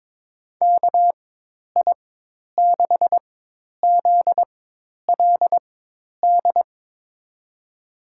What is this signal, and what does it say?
Morse code, heard as a keyed tone, about 22 wpm, spaced as for 10 wpm, 700 Hz. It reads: KI6ZLD
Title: USA Call Signs - 4x Character Spacing - 22wpm